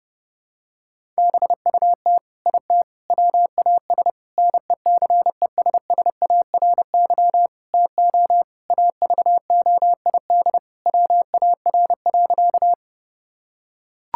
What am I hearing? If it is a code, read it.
Morse code, heard as a keyed tone, about 30 wpm, 700 Hz. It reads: BUT IT WAS NECESSARY TO AVOID WAR.
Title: Sentences from Top 1000 Words - Review Part 6 - 30wpm